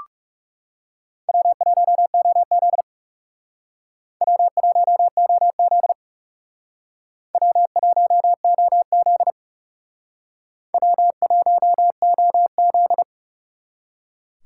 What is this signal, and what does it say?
Morse code, heard as a keyed tone, about 39 wpm, 700 Hz. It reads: W1OZ W1OZ W1OZ W1OZ